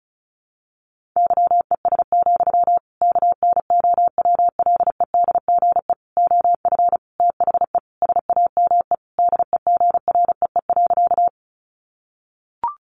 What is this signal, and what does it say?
Morse code, heard as a keyed tone, about 34 wpm, 700 Hz. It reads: YES, KNOWLEDGE OF THE SAME DEGREE.
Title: Sentences from Top 1500 Words 35wpm